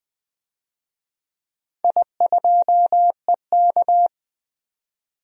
Morse code, heard as a keyed tone, about 20 wpm, 700 Hz. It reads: I2EK
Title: DXCC-100 Call Signs - 4 Characters - 20wpm